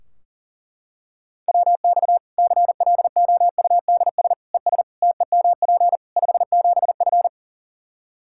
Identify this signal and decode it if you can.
Morse code, heard as a keyed tone, about 40 wpm, 700 Hz. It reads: WX CLOUDS ES TEMP 57F
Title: Practice QSO - Exchange 2 - Info - 40wpm